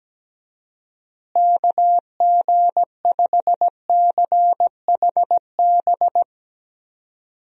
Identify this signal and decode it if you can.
Morse code, heard as a keyed tone, about 17 wpm, 700 Hz. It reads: KG5CHB
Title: USA Call Signs - 6 Characters - Repeated 2x - 17wpm